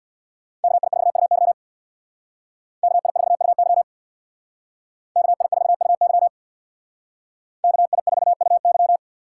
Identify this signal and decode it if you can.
Morse code, heard as a keyed tone, about 63 wpm, 700 Hz. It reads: KI4UY KI4UY KI4UY KI4UY